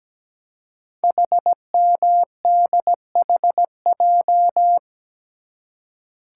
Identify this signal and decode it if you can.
Morse code, heard as a keyed tone, about 17 wpm, 700 Hz. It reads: HMDHJ
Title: Code Groups - 5 Letters - 17wpm